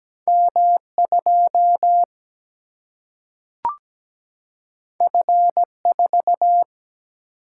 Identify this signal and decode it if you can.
Morse code, heard as a keyed tone, about 17 wpm, 700 Hz. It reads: M2 F4
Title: Single-Letter Single-Number 17wpm